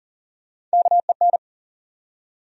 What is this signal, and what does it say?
Morse code, heard as a keyed tone, about 40 wpm, 700 Hz. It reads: KEN
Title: USA Common Names - Max 3 Letters - 40wpm